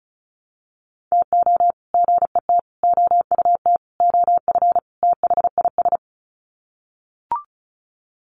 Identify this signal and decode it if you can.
Morse code, heard as a keyed tone, about 34 wpm, 700 Hz. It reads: TO GET OUT OF THIS
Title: Sets of 5 Words Encore2 35wpm